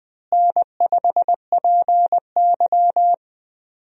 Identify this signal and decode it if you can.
Morse code, heard as a keyed tone, about 20 wpm, 700 Hz. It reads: N5PY
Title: USA Call Signs - 4 Characters - Repeated 3x - 20wpm